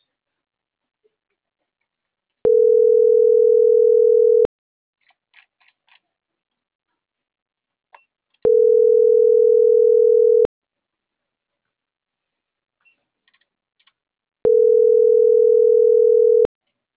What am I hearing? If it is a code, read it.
Ringback tone (North America), 440 + 480 Hz.
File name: CENTRAL MABE / out karen.maya